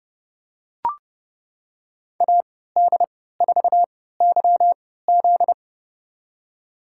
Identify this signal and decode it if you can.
Morse code, heard as a keyed tone, about 30 wpm, spaced as for 17 wpm, 700 Hz. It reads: AD4YZ
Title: USA Call Signs - 5 Characters - 3x Character Spacing - 30wpm